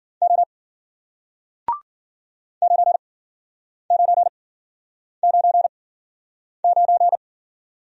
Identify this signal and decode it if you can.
Morse code, heard as a keyed tone, about 53 wpm, spaced as for 9 wpm, 700 Hz. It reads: O 9999